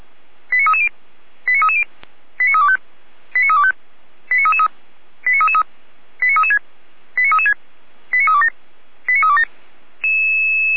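Five-tone selective call (ZVEI), heard as quick runs of five tones, tones of about 70 ms. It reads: 89330 89330 89325 89325 89333 89333 89336 89336 89327 89327
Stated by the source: Received on SDR; station feuerwehr-orlamuende.org (fire brigade Orlamünde).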